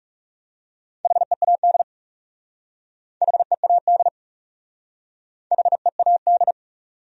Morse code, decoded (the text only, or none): HEAD HEAD HEAD